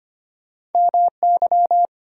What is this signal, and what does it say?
Morse code, heard as a keyed tone, about 33 wpm, 700 Hz. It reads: MY